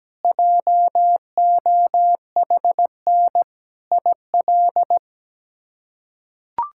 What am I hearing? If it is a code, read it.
Morse code, heard as a keyed tone, about 17 wpm, 700 Hz. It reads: JOHN IL
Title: K1USN SST QSO Practice - 17wpm